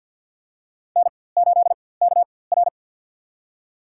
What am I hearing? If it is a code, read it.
Morse code, heard as a keyed tone, about 51 wpm, spaced as for 23 wpm, 700 Hz. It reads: N8KR
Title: USA Call Signs - 4x Character Spacing - 50wpm